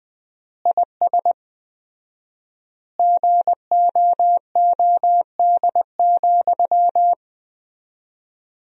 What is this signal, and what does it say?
Morse code, heard as a keyed tone, about 20 wpm, 700 Hz. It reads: IS GOOD,
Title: 4x Word-Spacing - Sentences from Top 200 Words - 20wpm